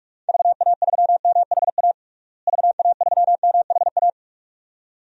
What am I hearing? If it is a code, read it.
Morse code, heard as a keyed tone, about 41 wpm, 700 Hz. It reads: VA3MHA VA3MHA
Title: DXCC-100 Call Signs - 6 Characters - Repeated 2x - 45wpm